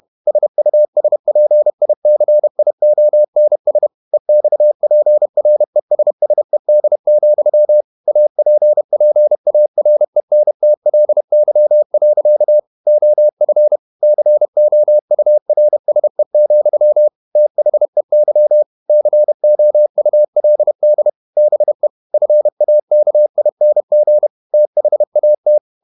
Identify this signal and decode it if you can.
Morse code, heard as a keyed tone, about 31 wpm, 600 Hz. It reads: SUSPICIONS EXPRESSED, APPARENTLY. OF COURSE, THEY COULD BE FAKING THAT
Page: Never 31wpm | CW med Gnister